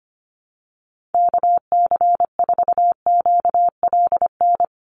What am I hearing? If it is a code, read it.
Morse code, heard as a keyed tone, about 25 wpm, 700 Hz. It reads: KC4QLN
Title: USA Call Signs 25wpm